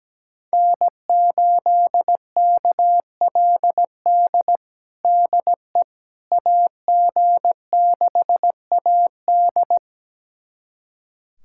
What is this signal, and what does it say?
Morse code, heard as a keyed tone, about 17 wpm, 700 Hz. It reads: N8KLD DE AG6AD
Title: USA Callsign DE Callsign - 5 Characters - 17wpm